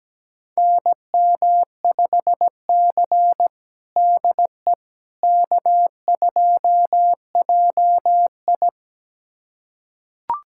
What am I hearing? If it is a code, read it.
Morse code, heard as a keyed tone, about 17 wpm, 700 Hz. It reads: NM5C DE K2JI
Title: USA Callsign DE Callsign - 4 Characters - 17wpm